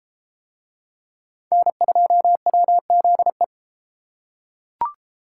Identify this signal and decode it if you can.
Morse code, heard as a keyed tone, about 30 wpm, 700 Hz. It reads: N2WZE